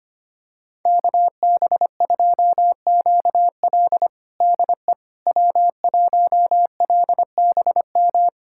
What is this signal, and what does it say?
Morse code, heard as a keyed tone, about 25 wpm, 700 Hz. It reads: KB2QL DE W1LBM
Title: USA Callsign DE Callsign - 5 Characters - 25wpm